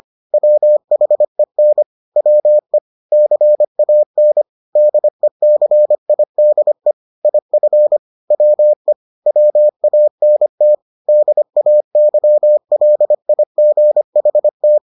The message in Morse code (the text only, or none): WHEN WE CAN DECIDE IF WE WANT DAYLIGHT